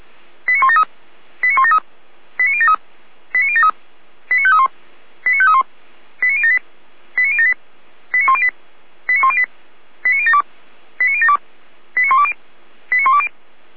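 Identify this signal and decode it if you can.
Five-tone selective call (ZVEI), heard as quick runs of five tones, tones of about 70 ms. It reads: 89173 89173 89073 89073 89531 89531 89078 89078 89108 89108 89072 89072 89120 89120
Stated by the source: ZVEI five-tone call received by SDR, from audio.f85360.feuerwehr-orlamuende.org fire brigade server.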